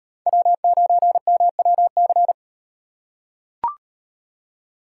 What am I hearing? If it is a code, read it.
Morse code, heard as a keyed tone, about 36 wpm, 700 Hz. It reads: W9MWC